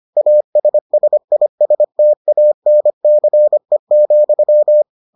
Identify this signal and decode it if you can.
Morse code, heard as a keyed tone, about 25 wpm, 600 Hz. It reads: ASSISTANCE,